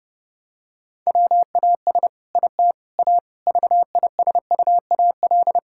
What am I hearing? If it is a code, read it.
Morse code, heard as a keyed tone, about 30 wpm, 700 Hz. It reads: WAS IT A VISUAL
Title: Sentences from Top 1000 Words - Review Part 6 - 30wpm